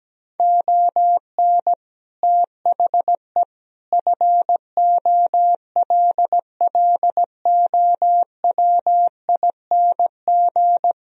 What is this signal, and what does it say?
Morse code, heard as a keyed tone, about 17 wpm, 700 Hz. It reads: ON THE FOLLOWING
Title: Sentences from Top 1000 Words - Review Part 4 - 17wpm